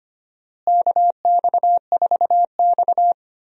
Morse code, heard as a keyed tone, about 25 wpm, 700 Hz. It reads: KX4X
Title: USA Call Signs - 4 Characters - Repeated 3x - 25wpm